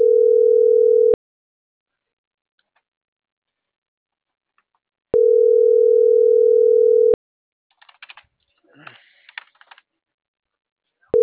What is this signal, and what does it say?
Ringback tone (North America), 440 + 480 Hz.